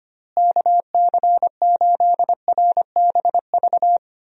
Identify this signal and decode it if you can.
Morse code, heard as a keyed tone, about 25 wpm, 700 Hz. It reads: KC8RBV
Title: USA Call Signs - 6 Characters - Repeated 2x - 25wpm